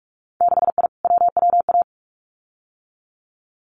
Morse code, heard as a keyed tone, about 46 wpm, 700 Hz. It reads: 6I WWA
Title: Winter Field Day Exchanges Encore 45wpm